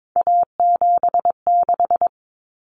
Morse code, heard as a keyed tone, about 22 wpm, 700 Hz. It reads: A76
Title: Common Cut Numbers - 3-Digit 22wpm